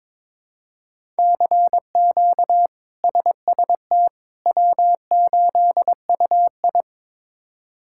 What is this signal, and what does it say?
Morse code, heard as a keyed tone, about 22 wpm, 700 Hz. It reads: CQ SST W8UI
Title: K1USN SST QSO Practice - 22wpm